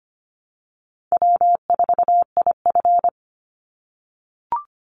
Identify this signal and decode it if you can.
Morse code, heard as a keyed tone, about 25 wpm, 700 Hz. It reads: W4IF
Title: Easy Call Signs - 25wpm